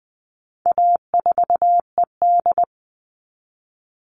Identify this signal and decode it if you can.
Morse code, heard as a keyed tone, about 20 wpm, 700 Hz. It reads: A4ED
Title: All Cut Numbers - 4-Digit 20wpm